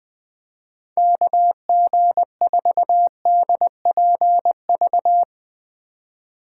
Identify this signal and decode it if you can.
Morse code, heard as a keyed tone, about 21 wpm, 700 Hz. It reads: KG4DPV
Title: USA Call Signs - 6 Characters - 3x Character Spacing - 20wpm